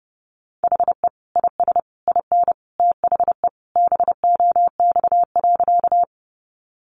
Morse code, heard as a keyed tone, about 30 wpm, 700 Hz. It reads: HE IS IN THE BOX.
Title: Sentences from Top 500 Words 30wpm